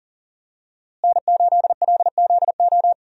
Morse code, heard as a keyed tone, about 40 wpm, 700 Hz. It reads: N8LZO
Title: USA Call Signs - 5 Characters - Repeated 3x - 40wpm